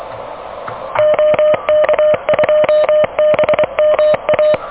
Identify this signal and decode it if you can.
Morse code, heard as a keyed tone, about 19 wpm, 610 Hz. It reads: OK2BMA
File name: OK QRP 2023 / OK QRP 2023 OK1DWF